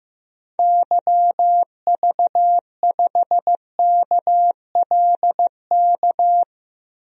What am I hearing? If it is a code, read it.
Morse code, heard as a keyed tone, about 15 wpm, 700 Hz. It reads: YV5KLK
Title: DXCC-100 Call Signs - 6 Characters - 15wpm